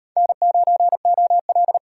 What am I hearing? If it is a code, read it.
Morse code, heard as a keyed tone, about 34 wpm, 700 Hz. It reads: N9OL